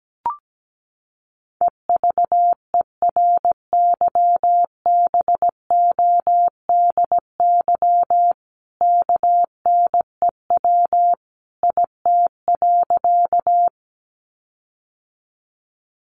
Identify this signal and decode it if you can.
Morse code, heard as a keyed tone, about 17 wpm, 700 Hz. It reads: EVERYBODY KNEW IT.
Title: Sentences from Top 1400 Words 17wpm